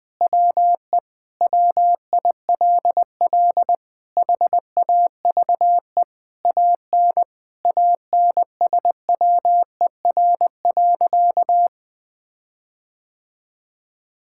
Morse code, heard as a keyed tone, about 20 wpm, 700 Hz. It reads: WE WILL HAVE AN ANSWER.
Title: Sentences with 5-Word Phrases - 20wpm